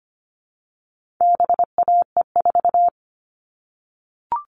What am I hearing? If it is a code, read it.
Morse code, heard as a keyed tone, about 25 wpm, 700 Hz. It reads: BAE4
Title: All Cut Numbers - 4-Digit 25wpm